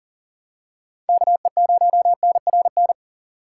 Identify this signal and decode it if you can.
Morse code, heard as a keyed tone, about 40 wpm, 700 Hz. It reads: KE0NRN
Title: USA Call Signs - 2x Character Spacing - 40wpm